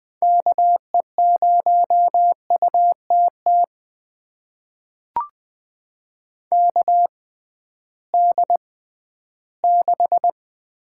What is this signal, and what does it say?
Morse code, heard as a keyed tone, about 20 wpm, spaced as for 13 wpm, 700 Hz. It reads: KE0UTT KD6
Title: USA Call Signs - 6 Characters - 6x Character Spacing - 20wpm